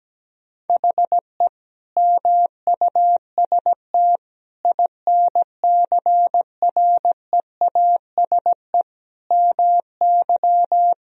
Morse code, heard as a keyed tone, about 17 wpm, 700 Hz. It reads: HE MUST INCREASE MY
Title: Sentences from Top 1000 Words - Review Part 4 - 17wpm